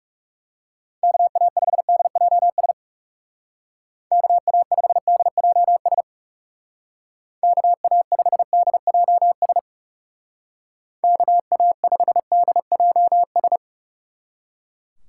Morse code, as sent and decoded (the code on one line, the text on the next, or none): -.- .- ..... -.. .--- ... / -.- .- ..... -.. .--- ... / -.- .- ..... -.. .--- ... / -.- .- ..... -.. .--- ...
KA5DJS KA5DJS KA5DJS KA5DJS